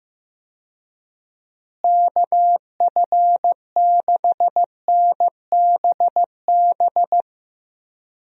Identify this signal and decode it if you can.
Morse code, heard as a keyed tone, about 15 wpm, 700 Hz. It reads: KF6NBB
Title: USA Call Signs - 6 Characters - Repeated 3x - 15wpm